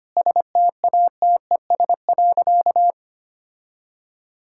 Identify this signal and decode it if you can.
Morse code, heard as a keyed tone, about 25 wpm, 700 Hz. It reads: STATES.
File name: practice / 25wpm / Sentences with 4-Word Phrases - 25wpm